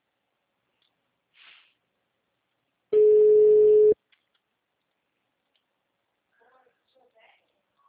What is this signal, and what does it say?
Ringback tone (Europe, 425 Hz).
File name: out- -2025- - - .47160